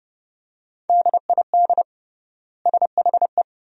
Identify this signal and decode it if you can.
Morse code, heard as a keyed tone, about 30 wpm, 700 Hz. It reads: DID SHE
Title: 3x Word-Spacing - Sentences from Top 500 Words - 30wpm